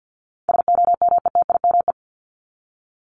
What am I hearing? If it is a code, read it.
Morse code, heard as a keyed tone, about 50 wpm, 700 Hz. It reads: SOMETIME
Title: ES:S Suffix - Top 100 Words with Base Words 50wpm